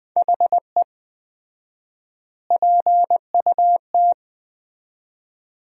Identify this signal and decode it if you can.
Morse code, heard as a keyed tone, about 21 wpm, 700 Hz. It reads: HE PUT
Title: 4x Word-Spacing - Sentences from Top 400 Words - 20wpm